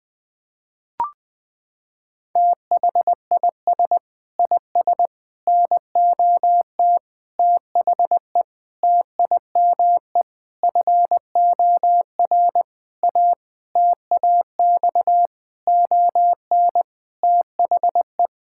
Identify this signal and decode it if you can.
Morse code, heard as a keyed tone, about 21 wpm, 700 Hz. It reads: THIS IS NOT THE TIME FOR A TAX ON THE
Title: Sentences from Top 1000 Words - Review Part 5 - 20wpm